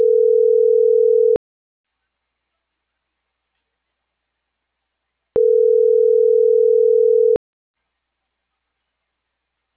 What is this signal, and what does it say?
Ringback tone (North America), 440 + 480 Hz.